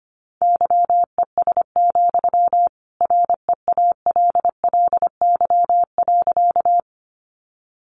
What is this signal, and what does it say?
Morse code, heard as a keyed tone, about 25 wpm, 700 Hz. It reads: YES, REALLY.
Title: Sentences from Top 600 Words 25wpm